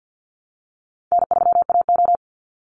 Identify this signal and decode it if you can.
Morse code, heard as a keyed tone, about 50 wpm, 700 Hz. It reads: N3AO
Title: Call Signs - 50wpm